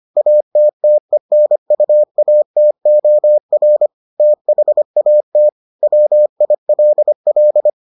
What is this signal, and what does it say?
Morse code, heard as a keyed tone, about 25 wpm, 600 Hz. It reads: ATTENUATOR THAT WILL